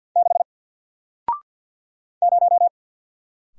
Morse code, heard as a keyed tone, about 49 wpm, 700 Hz. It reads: 6 0
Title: Single Letter-Number Alt 50wpm